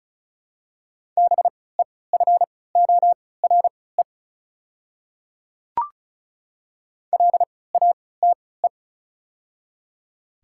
Morse code, heard as a keyed tone, about 34 wpm, spaced as for 20 wpm, 700 Hz. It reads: BEFORE LATE
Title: 3x Character Spacing - Top 300 Words - 35wpm